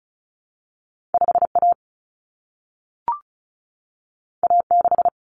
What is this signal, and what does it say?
Morse code, heard as a keyed tone, about 36 wpm, 700 Hz. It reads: 5A A6
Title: Common Cut Numbers - 2-Digit 35wpm